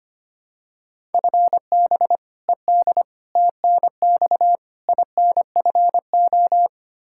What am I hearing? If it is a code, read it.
Morse code, heard as a keyed tone, about 25 wpm, 700 Hz. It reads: FB ED TNX INFO
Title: Practice QSO - Exchange 3 - More Info - 25wpm